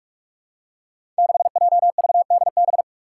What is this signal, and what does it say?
Morse code, heard as a keyed tone, about 44 wpm, 700 Hz. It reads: 6JVDB
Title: Five Challenging Letter-Number Combinations 45wpm